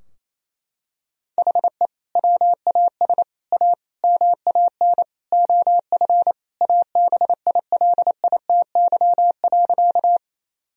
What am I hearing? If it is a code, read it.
Morse code, heard as a keyed tone, about 28 wpm, 700 Hz. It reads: HE WAS A MAN OF ABILITY.
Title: Sentences from Top 1000 Words - Review Part 3 - 28wpm